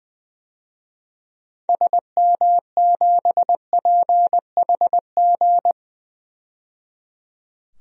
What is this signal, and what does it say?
Morse code, heard as a keyed tone, about 20 wpm, 700 Hz. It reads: SM7PHG
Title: DXCC-100 Call Signs - 6 Characters - 20wpm